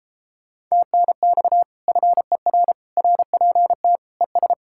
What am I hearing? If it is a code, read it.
Morse code, heard as a keyed tone, about 34 wpm, 700 Hz. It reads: TNX FER RPT ES
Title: Speed-Racing Common QSO Phrases - 28wpm